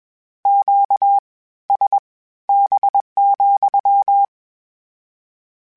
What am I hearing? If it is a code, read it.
Morse code, heard as a keyed tone, about 21 wpm, spaced as for 14 wpm, 800 Hz. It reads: QSB,